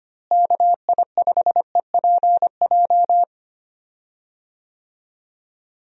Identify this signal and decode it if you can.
Morse code, heard as a keyed tone, about 25 wpm, 700 Hz. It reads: KI5EPJ
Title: USA Call Signs - 6 Characters - Repeated 2x - 25wpm